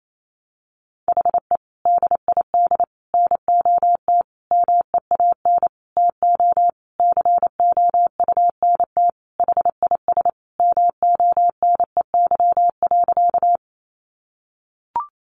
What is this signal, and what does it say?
Morse code, heard as a keyed tone, about 28 wpm, 700 Hz. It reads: HE DID NOT MEAN TO COUNT HIS MONEY.
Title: Sentences from Top 1500 Words 28wpm